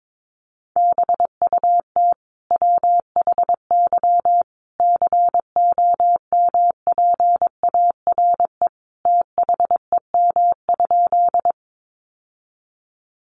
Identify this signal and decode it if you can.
Morse code, heard as a keyed tone, about 22 wpm, 700 Hz. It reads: BUT WHY COMPARE THEM?
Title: Sentences from Top 1300 Words 22wpm